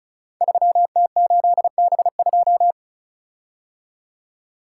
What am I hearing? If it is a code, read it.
Morse code, heard as a keyed tone, about 35 wpm, 690 Hz. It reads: 3T8B2
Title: Code Groups - 5 Letter:Numbers - 35wpm